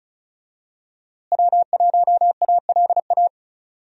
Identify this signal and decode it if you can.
Morse code, heard as a keyed tone, about 34 wpm, 700 Hz. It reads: W1ALA
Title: USA Call Signs - 5 Characters - Repeated 3x - 35wpm